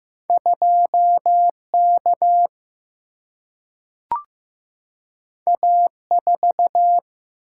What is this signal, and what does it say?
Morse code, heard as a keyed tone, about 15 wpm, 700 Hz. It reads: 2 K A 4
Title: Single-Letter Single-Number 15wpm